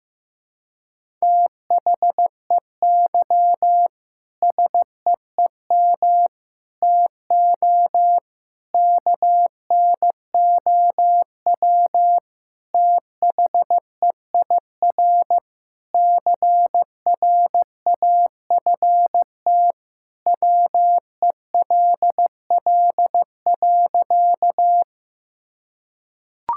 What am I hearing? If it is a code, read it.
Morse code, heard as a keyed tone, about 15 wpm, 700 Hz. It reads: THEY SEEM TO KNOW THEIR CRAFT WELL.
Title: Sentences from Top 1000 Words - Review Part 6 - 15wpm